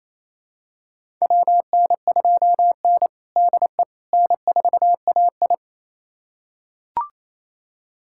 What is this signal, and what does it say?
Morse code, heard as a keyed tone, about 27 wpm, 700 Hz. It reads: WN2N DE N4AI
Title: USA Callsign DE Callsign - 4 Characters - 28wpm